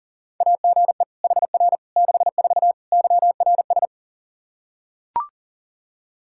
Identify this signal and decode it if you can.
Morse code, heard as a keyed tone, about 37 wpm, 700 Hz. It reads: AGE HR 64 YRS
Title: K1USN SST QSO Practice 40wpm